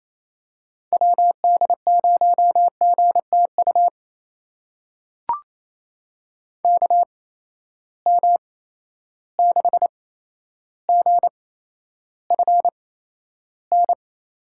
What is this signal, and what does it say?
Morse code, heard as a keyed tone, about 28 wpm, spaced as for 12 wpm, 700 Hz. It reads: WD0GTU KM6GFN
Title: USA Call Signs - 6 Characters - 8x Character Spacing - 28wpm